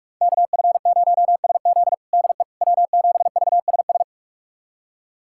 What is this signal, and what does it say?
Morse code, heard as a keyed tone, about 46 wpm, 700 Hz. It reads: KF0SZ DE W7VSS
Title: USA Callsign DE Callsign - 5 Characters - 45wpm